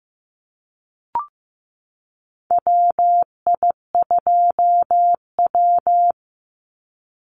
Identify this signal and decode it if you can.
Morse code, heard as a keyed tone, about 15 wpm, 700 Hz. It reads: WI2W
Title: NA SOTA Chaser Callsigns 15wpm